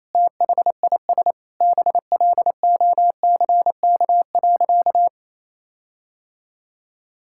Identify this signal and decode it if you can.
Morse code, heard as a keyed tone, about 27 wpm, 700 Hz. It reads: THIS BLOCK.
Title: Sentences from Top 1600 Words 28wpm